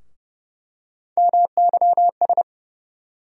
Morse code, heard as a keyed tone, about 30 wpm, 700 Hz. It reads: MYS
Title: USA Call Sign Postfixes - 3 Letters 30wpm